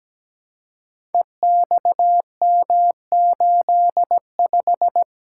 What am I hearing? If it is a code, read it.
Morse code, heard as a keyed tone, about 17 wpm, 700 Hz. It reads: EXM85
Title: Code Groups - 5 Letter:Numbers - 17wpm